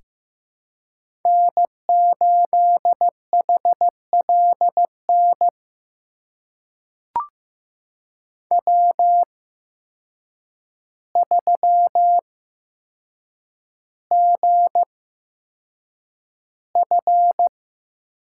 Morse code, heard as a keyed tone, about 15 wpm, spaced as for 7 wpm, 700 Hz. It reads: N8HLN W3GF
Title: USA Call Signs - 8x Character Spacing - 15wpm